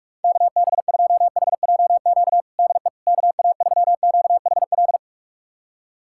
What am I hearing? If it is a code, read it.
Morse code, heard as a keyed tone, about 46 wpm, 690 Hz. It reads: KB2HJQ DE KA3QHL
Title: USA Callsign DE Callsign 45wpm